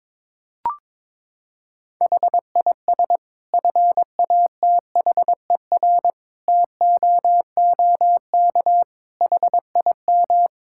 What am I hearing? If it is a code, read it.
Morse code, heard as a keyed tone, about 22 wpm, 700 Hz. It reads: HIS FATHER TOOK HIM
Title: Sentences from Top 1000 Words - Review Part 5 - 22wpm